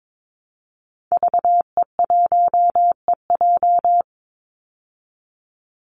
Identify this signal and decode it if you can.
Morse code, heard as a keyed tone, about 22 wpm, 700 Hz. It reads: VE1EJ
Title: Call Signs Repeated 3x - 22wpm